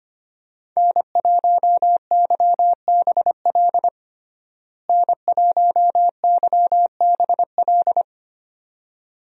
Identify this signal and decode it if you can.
Morse code, heard as a keyed tone, about 25 wpm, 700 Hz. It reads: N1YBL N1YBL
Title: USA Call Signs - 5 Characters - Repeated 2x - 25wpm